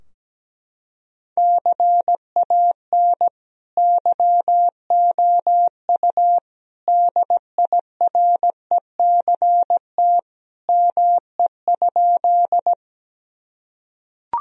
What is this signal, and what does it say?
Morse code, heard as a keyed tone, about 17 wpm, 700 Hz. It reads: CAN YOU DIRECT ME?
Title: Sentences from Top 1000 Words - Review Part 2 - 17wpm